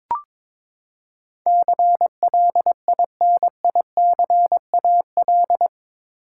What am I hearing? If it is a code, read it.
Morse code, heard as a keyed tone, about 22 wpm, 700 Hz. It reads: CLINICAL
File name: practice / 22wpm / Eight Letter Words - 22wpm